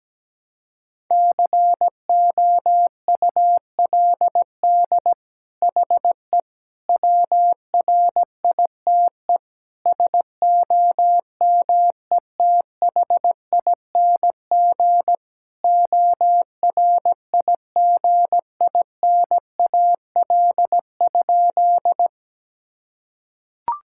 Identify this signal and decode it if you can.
Morse code, heard as a keyed tone, about 17 wpm, 700 Hz. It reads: COULD HE WRITE SOMETHING ORIGINAL?
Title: Sentences from Top 1000 Words - Review Part 6 - 17wpm